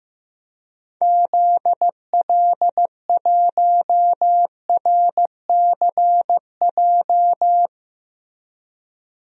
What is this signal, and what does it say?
Morse code, heard as a keyed tone, about 15 wpm, 700 Hz. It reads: ZL1RCJ
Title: DXCC-100 Call Signs - 6 Characters - Repeated 2x - 15wpm